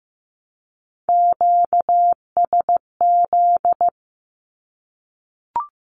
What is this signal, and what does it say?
Morse code, heard as a keyed tone, about 15 wpm, 700 Hz. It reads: QSZ
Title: Q-Codes Repeated 3x 15wpm